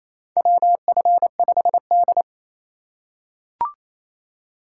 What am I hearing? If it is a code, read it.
Morse code, heard as a keyed tone, about 27 wpm, 700 Hz. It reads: WF5D
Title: USA Call Signs - 4 Characters - 4x Character Spacing - 28wpm